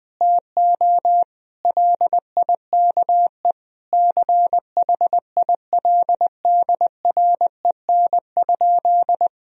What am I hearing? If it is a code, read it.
Morse code, heard as a keyed tone, about 21 wpm, 700 Hz. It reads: TO LIKE CHILDREN?
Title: Sentences from Top 1000 Words - Review Part 2 - 20wpm